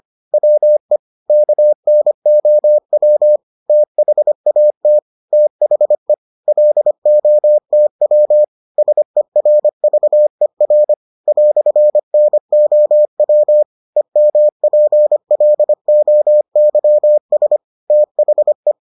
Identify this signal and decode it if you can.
Morse code, heard as a keyed tone, about 25 wpm, 600 Hz. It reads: WE KNOW THAT THE LOTW SERVER "NOW EMPLOYS THE